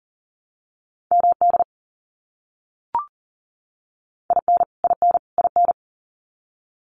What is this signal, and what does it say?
Morse code, heard as a keyed tone, about 40 wpm, 700 Hz. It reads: MD IN IN IN
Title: US States Repeated 3x 40wpm